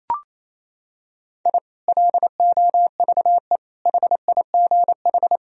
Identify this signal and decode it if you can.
Morse code, heard as a keyed tone, about 28 wpm, 700 Hz. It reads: I LOVE HIGH
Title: Sentences from Top 1000 Words - Review Part 4 - 28wpm